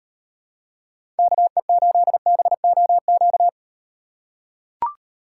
Morse code, heard as a keyed tone, about 38 wpm, 700 Hz. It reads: KE8BOQ